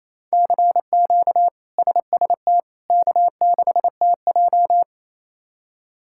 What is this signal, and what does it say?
Morse code, heard as a keyed tone, about 27 wpm, 700 Hz. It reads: CQ SST K6TJ
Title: K1USN SST QSO Practice - 28wpm